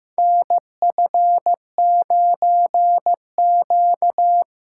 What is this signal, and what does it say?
Morse code, heard as a keyed tone, about 15 wpm, 700 Hz. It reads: NF9Q
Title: USA Call Signs - 4 Characters - 4x Character Spacing - 15wpm